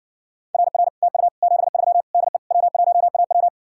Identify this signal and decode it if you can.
Morse code, heard as a keyed tone, about 68 wpm, 700 Hz. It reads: RR TU 73 DE W1AW